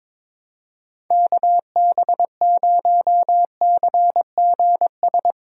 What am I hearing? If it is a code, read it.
Morse code, heard as a keyed tone, about 22 wpm, 700 Hz. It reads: KB0CGS
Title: USA Call Signs - 6 Characters - Repeated 2x - 22wpm